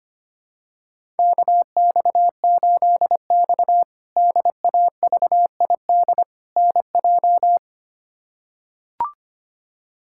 Morse code, heard as a keyed tone, about 25 wpm, 700 Hz. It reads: KX8X DAVID NJ
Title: K1USN SST QSO Practice - 25wpm